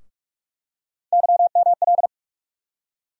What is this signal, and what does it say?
Morse code, heard as a keyed tone, about 45 wpm, 700 Hz. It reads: YML
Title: USA Call Sign Postfixes - 3 Letters 45wpm